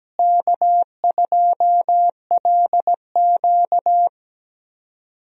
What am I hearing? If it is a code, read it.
Morse code, heard as a keyed tone, about 17 wpm, 700 Hz. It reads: K2LQ
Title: USA Call Signs - 4 Characters - Repeated 3x - 17wpm